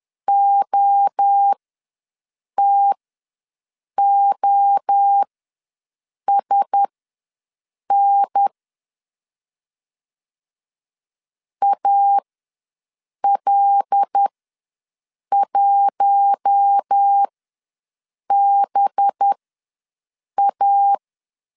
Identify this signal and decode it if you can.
Morse code, heard as a keyed tone, about 10 wpm, spaced as for 6 wpm, 800 Hz. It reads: OTOSN AL1BA
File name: LEZIONI / 5 LBVC / 5 LB ws 10wpm 3616